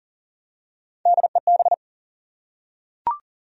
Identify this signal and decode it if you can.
Morse code, heard as a keyed tone, about 40 wpm, 700 Hz. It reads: DEB
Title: USA Common Names - Max 3 Letters - 40wpm